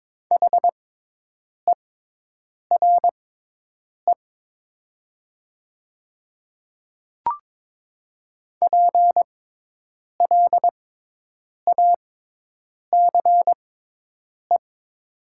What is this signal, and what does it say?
Morse code, heard as a keyed tone, about 22 wpm, spaced as for 8 wpm, 700 Hz. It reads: HERE PLACE
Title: 6x Character Spacing - Top 300 Words - 22wpm